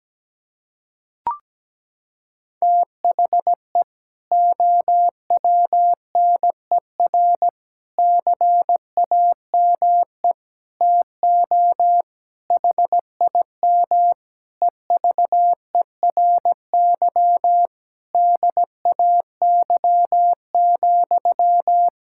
Morse code, heard as a keyed tone, about 17 wpm, 700 Hz. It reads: THE OWNER CAME TO HIM EVERY DAY,
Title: Sentences from Top 1600 Words 17wpm